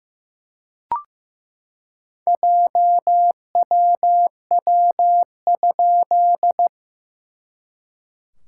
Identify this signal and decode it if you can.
Morse code, heard as a keyed tone, about 15 wpm, 700 Hz. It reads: JWW?
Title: SOTA QSO Practice 15wpm